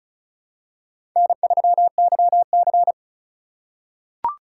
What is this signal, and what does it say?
Morse code, heard as a keyed tone, about 34 wpm, 700 Hz. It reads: N3YC
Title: USA Call Signs - 4x Character Spacing - 35wpm